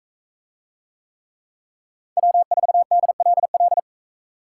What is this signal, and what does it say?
Morse code, heard as a keyed tone, about 42 wpm, 700 Hz. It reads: W4DLL